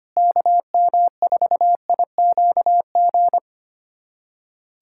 Morse code, heard as a keyed tone, about 25 wpm, 700 Hz. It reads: KM4IQG
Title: USA Callsign DE Callsign - 6 Characters - 25wpm